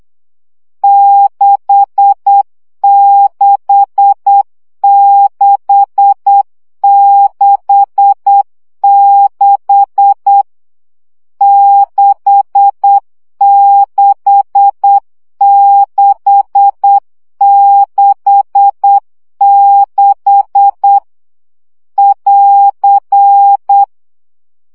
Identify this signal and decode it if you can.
Morse code, heard as a keyed tone, about 8 wpm, 800 Hz. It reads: 66666 66666 <AR>
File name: CW gyakorlas 6 7 baud